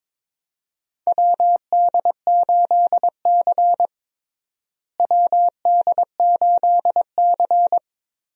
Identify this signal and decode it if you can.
Morse code, heard as a keyed tone, about 22 wpm, 700 Hz. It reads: WD8C WD8C
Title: USA Call Signs - 4 Characters - Repeated 2x - 22wpm